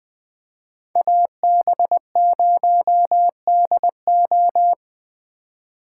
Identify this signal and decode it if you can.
Morse code, heard as a keyed tone, about 20 wpm, 700 Hz. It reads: AB0DO